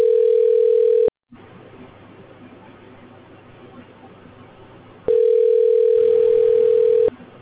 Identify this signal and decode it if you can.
Ringback tone (North America), 440 + 480 Hz.